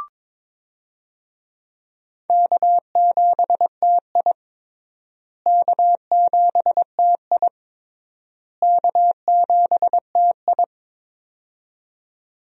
Morse code, heard as a keyed tone, about 22 wpm, 700 Hz. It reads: K7TI K7TI K7TI
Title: USA Call Signs - 4 Characters - Repeated 3x - 22wpm